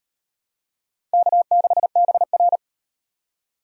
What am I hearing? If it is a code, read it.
Morse code, heard as a keyed tone, about 36 wpm, 700 Hz. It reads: K6BR